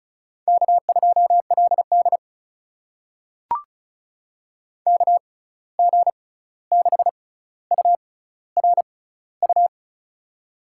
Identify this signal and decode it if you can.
Morse code, heard as a keyed tone, about 36 wpm, spaced as for 16 wpm, 700 Hz. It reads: K2LD KG6URU
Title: USA Call Signs - 6x Character Spacing - 35wpm